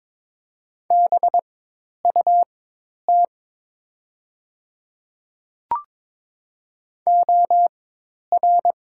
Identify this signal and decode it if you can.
Morse code, heard as a keyed tone, about 22 wpm, spaced as for 10 wpm, 700 Hz. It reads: BUT OR
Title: 4x Character Spacing - Top 100 Words - 22wpm